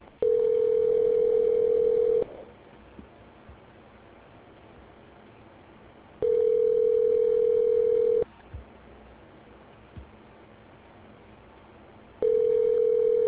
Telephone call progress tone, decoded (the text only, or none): ringback tone